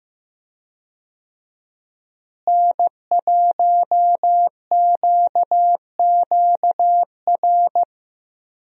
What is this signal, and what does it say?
Morse code, heard as a keyed tone, about 15 wpm, 700 Hz. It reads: N1QQR
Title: USA Call Signs - 5 Characters - Repeated 2x - 15wpm